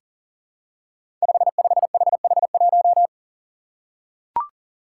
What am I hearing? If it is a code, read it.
Morse code, heard as a keyed tone, about 38 wpm, 700 Hz. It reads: 55HH1
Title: Five Challenging Letter-Number Combinations 40wpm